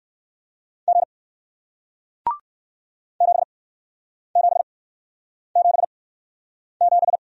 Morse code, heard as a keyed tone, about 61 wpm, spaced as for 9 wpm, 700 Hz. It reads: G 7777